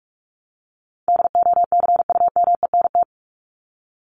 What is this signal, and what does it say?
Morse code, heard as a keyed tone, about 44 wpm, 700 Hz. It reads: DOCUMENT
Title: Top 1751-2000 Words 45wpm